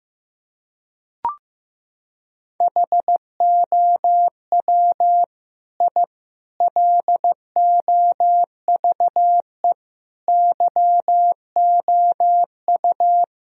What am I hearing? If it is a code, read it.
Morse code, heard as a keyed tone, about 15 wpm, 700 Hz. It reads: HOW I LOVE YOU
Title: Sentences from Top 1000 Words - Review Part 1 - 15wpm